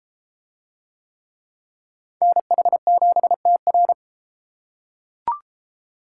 Morse code, heard as a keyed tone, about 33 wpm, 700 Hz. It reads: NH7TR